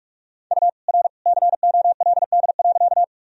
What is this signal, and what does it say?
Morse code, heard as a keyed tone, about 46 wpm, 700 Hz. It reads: U R COLD.
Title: Sentences from Top 400 Words and Abbreviations 45wpm